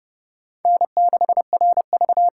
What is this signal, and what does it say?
Morse code, heard as a keyed tone, about 30 wpm, 700 Hz. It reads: N6RV
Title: USA Call Signs - 5 Characters - 8x Character Spacing - 30wpm